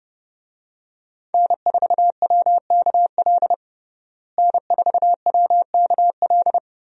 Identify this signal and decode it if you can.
Morse code, heard as a keyed tone, about 30 wpm, 700 Hz. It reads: N4WKL N4WKL
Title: USA Call Signs - 5 Characters - Repeated 3x - 30wpm